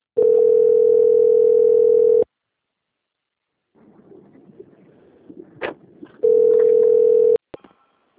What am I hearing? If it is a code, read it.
Ringback tone (North America), 440 + 480 Hz.
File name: out- - - - - .8393